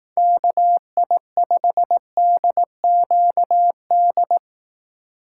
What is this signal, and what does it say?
Morse code, heard as a keyed tone, about 18 wpm, 700 Hz. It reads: KI5DQD
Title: Speed Racing - USA Call Signs 15wpm